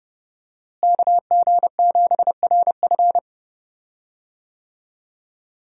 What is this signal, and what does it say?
Morse code, heard as a keyed tone, about 30 wpm, 700 Hz. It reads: KG7RF
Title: USA Call Signs - 5 Characters - Repeated 3x - 30wpm